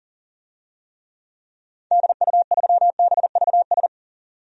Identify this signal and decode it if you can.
Morse code, heard as a keyed tone, about 40 wpm, 700 Hz. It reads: DU3BVS
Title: DXCC-100 Call Signs - 6 Characters - 40wpm